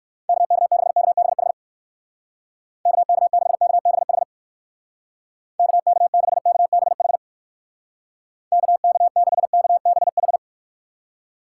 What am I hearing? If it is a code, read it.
Morse code, heard as a keyed tone, about 56 wpm, 700 Hz. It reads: KK6KBH KK6KBH KK6KBH KK6KBH
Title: Speed Racing - USA 2x3 Call Signs 45wpm